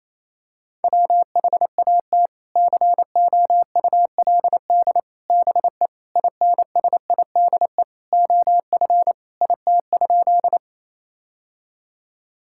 Morse code, heard as a keyed tone, about 28 wpm, 700 Hz. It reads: WHAT COULD BE INSIDE OF IT?
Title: Sentences from Top 1000 Words - Review Part 6 - 28wpm